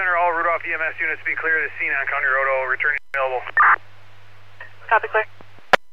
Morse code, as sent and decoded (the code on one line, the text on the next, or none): none